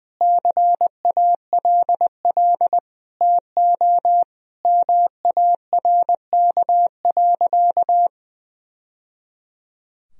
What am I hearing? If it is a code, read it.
Morse code, heard as a keyed tone, about 20 wpm, 700 Hz. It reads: CALL TO MARK.
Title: Sentences from Top 1000 Words - Review Part 2 - 20wpm